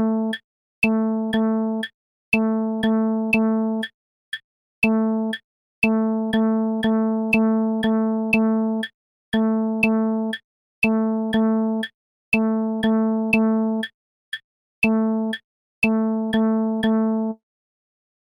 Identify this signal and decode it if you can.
There is no code.